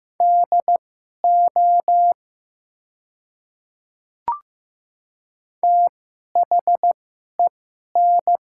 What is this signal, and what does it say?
Morse code, heard as a keyed tone, about 15 wpm, spaced as for 11 wpm, 700 Hz. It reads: DO THEN
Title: 2x Character Spacing - Top 200 Words - 15wpm